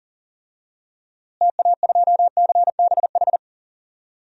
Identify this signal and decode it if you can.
Morse code, heard as a keyed tone, about 40 wpm, 700 Hz. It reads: TA2CBH
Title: DXCC-100 Call Signs - 6 Characters - Repeated 2x - 40wpm